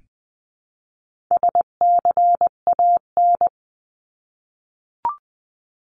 Morse code, heard as a keyed tone, about 20 wpm, 700 Hz. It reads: SCAN
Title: 259 Lesson 42d - Words 18-20wpm